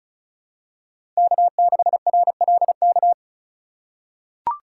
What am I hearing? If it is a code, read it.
Morse code, heard as a keyed tone, about 36 wpm, 700 Hz. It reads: K6RLK
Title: USA Call Signs - 5 Characters - 3x Character Spacing - 35wpm